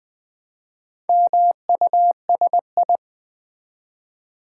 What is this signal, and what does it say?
Morse code, heard as a keyed tone, about 20 wpm, 700 Hz. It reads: MUSI
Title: Build A Word - 1 Letter At A Time - 5-Letter Words - 20wpm